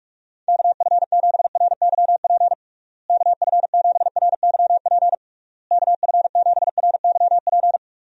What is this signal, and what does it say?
Morse code, heard as a keyed tone, about 43 wpm, 700 Hz. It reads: KF7RYP KF7RYP KF7RYP
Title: USA Call Signs - 6 Characters - Repeated 3x - 45wpm